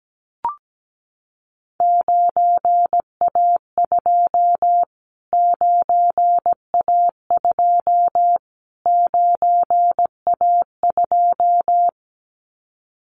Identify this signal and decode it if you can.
Morse code, heard as a keyed tone, about 17 wpm, 700 Hz. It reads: 9A2 9A2 9A2
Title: Callsign Prefixes for Top 25 DXCC - Repeated 3x - 17wpm